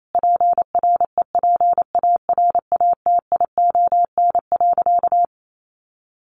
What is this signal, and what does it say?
Morse code, heard as a keyed tone, about 28 wpm, 700 Hz. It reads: PREPARATION.
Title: Sentences from Top 1400 Words 28wpm